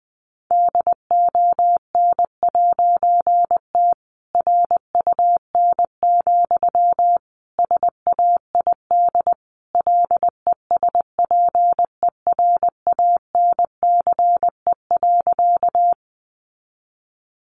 Morse code, heard as a keyed tone, about 20 wpm, 700 Hz. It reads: DON'T RUN, SAID LESPERANCE.
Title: A Sound Of Thunder 20wpm